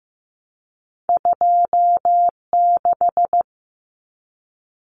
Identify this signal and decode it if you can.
Morse code, heard as a keyed tone, about 15 wpm, 700 Hz. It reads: 26
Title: Common Cut Numbers - 2-Digit 15wpm